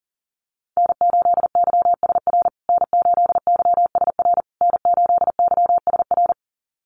Morse code, heard as a keyed tone, about 40 wpm, 700 Hz. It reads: N8YSR N8YSR N8YSR
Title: USA Call Signs Repeated 3x 40wpm